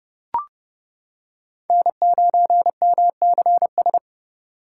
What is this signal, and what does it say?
Morse code, heard as a keyed tone, about 29 wpm, 700 Hz. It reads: N9MCS